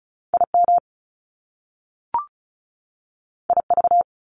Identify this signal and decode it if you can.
Morse code, heard as a keyed tone, about 36 wpm, 700 Hz. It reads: IM IV
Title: Top Two Letter Combinations in Words 35wpm